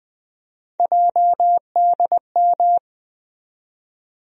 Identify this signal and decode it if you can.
Morse code, heard as a keyed tone, about 20 wpm, 700 Hz. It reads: JDM
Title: USA Call Sign Postfixes - 3 Letters 20wpm